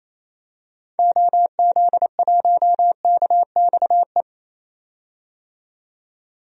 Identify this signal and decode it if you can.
Morse code, heard as a keyed tone, about 28 wpm, 700 Hz. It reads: OZ1KXE
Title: DXCC-100 Call Signs - 6 Characters - 28wpm